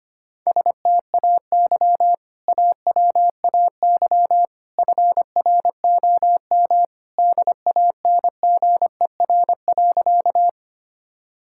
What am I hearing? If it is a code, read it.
Morse code, heard as a keyed tone, about 25 wpm, 700 Hz. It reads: STAY AWAY FROM DANGER.
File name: practice / 25wpm / Sentences with 2-Word Phrases - 25wpm